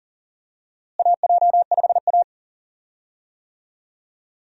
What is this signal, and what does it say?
Morse code, heard as a keyed tone, about 40 wpm, 700 Hz. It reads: AJ5A
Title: USA Call Signs - 4 Characters - Repeated 2x - 40wpm